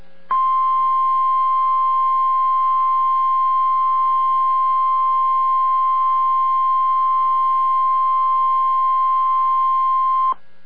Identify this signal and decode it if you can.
Emergency Alert System (SAME) tones (the 1050 Hz attention tone). It is heard from 0.3 to 10.4 s.